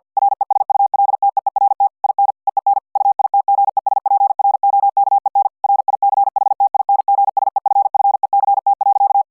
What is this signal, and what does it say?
Morse code, heard as a keyed tone, about 50 wpm, 800 Hz. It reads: RESULTEERT IN EEN UITGESPROKEN RICHTINGSEFFECT.